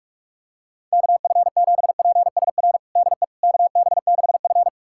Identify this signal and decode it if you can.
Morse code, heard as a keyed tone, about 47 wpm, 690 Hz. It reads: KF7PSR DE KB6F
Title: USA Callsign DE Callsign 45wpm